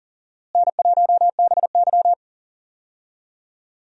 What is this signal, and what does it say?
Morse code, heard as a keyed tone, about 40 wpm, 700 Hz. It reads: N1BY
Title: USA Call Signs - 4 Characters - Repeated 2x - 40wpm